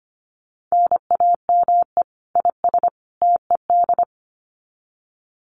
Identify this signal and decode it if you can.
Morse code, heard as a keyed tone, about 25 wpm, 700 Hz. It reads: NAME IS TED
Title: Common QSO Phrases 25wpm